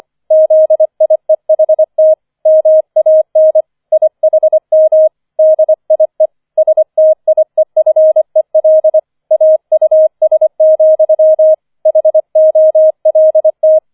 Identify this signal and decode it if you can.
Morse code, heard as a keyed tone, about 25 wpm, 630 Hz. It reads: ZIEHT MAN IHM DIE STIEFEL AUS, HOLT
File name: 005 Fünfter Streich 25wpm